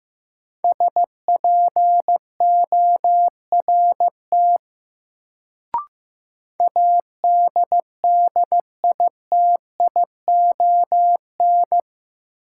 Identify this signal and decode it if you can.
Morse code, heard as a keyed tone, about 15 wpm, 700 Hz. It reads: SPORT ADDITION
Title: Sentences from Top 1500 Words - New Word Drill Alt - 15wpm